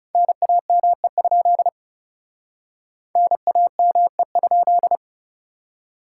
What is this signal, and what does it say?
Morse code, heard as a keyed tone, about 37 wpm, 700 Hz. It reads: NAME? NAME?
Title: Speed-Racing Common QSO Phrases - 30wpm